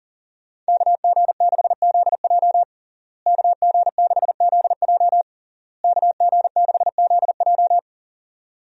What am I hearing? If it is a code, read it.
Morse code, heard as a keyed tone, about 37 wpm, 700 Hz. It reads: KG6ZJ KG6ZJ KG6ZJ
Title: USA Call Signs - 5 Characters - Repeated 3x - 40wpm